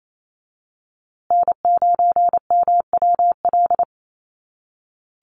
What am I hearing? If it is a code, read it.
Morse code, heard as a keyed tone, about 28 wpm, 700 Hz. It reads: N9MWL
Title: USA Call Signs Repeated 2x 28wpm